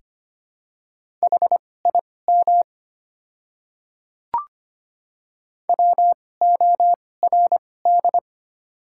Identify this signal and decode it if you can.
Morse code, heard as a keyed tone, about 25 wpm, spaced as for 18 wpm, 700 Hz. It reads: HIM WORD
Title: 2x Character Spacing - Top 100 Words - 25wpm